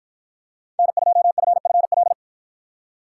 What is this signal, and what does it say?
Morse code, heard as a keyed tone, about 53 wpm, 700 Hz. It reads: N2FFL